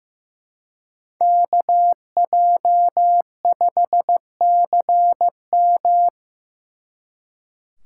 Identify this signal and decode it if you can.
Morse code, heard as a keyed tone, about 18 wpm, 700 Hz. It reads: KJ5CM
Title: Speed Racing - USA 2x2 Call Signs 15wpm